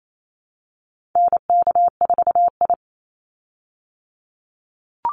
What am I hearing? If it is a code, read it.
Morse code, heard as a keyed tone, about 28 wpm, 700 Hz. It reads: NK4I
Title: Call Signs Repeated 3x - 28wpm